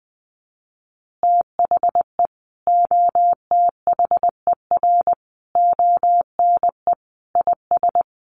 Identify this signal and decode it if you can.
Morse code, heard as a keyed tone, about 20 wpm, 700 Hz. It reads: THE OTHER ONE IS
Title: Additional Easy Phrases 20wpm